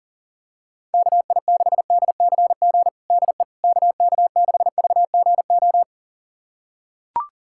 Morse code, heard as a keyed tone, about 37 wpm, 700 Hz. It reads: KI6DCG DE KK6VGO
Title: USA Callsign DE Callsign - 6 Characters - 40wpm